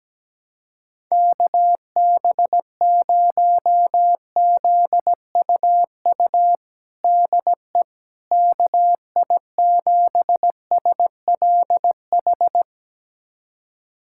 Morse code, heard as a keyed tone, about 17 wpm, 700 Hz. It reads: KB0ZUU DE KI7SLH
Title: USA Callsign DE Callsign - 6 Characters - 17wpm